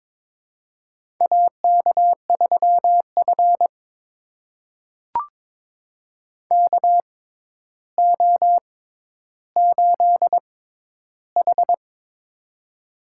Morse code, heard as a keyed tone, about 22 wpm, spaced as for 11 wpm, 700 Hz. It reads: AK3F KO8H
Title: USA Call Signs - 6x Character Spacing - 22wpm